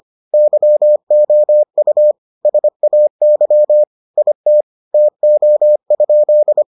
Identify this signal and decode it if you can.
Morse code, heard as a keyed tone, about 25 wpm, 600 Hz. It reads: YOU SAY IT TO?